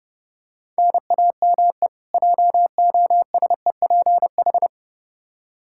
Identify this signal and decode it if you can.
Morse code, heard as a keyed tone, about 30 wpm, 700 Hz. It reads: NAME JOSEPH
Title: K1USN SST QSO Practice 30wpm